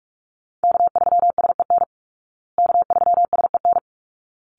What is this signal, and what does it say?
Morse code, heard as a keyed tone, about 44 wpm, 700 Hz. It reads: K3SEN K3SEN
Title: Call Signs Repeated 3x - 45wpm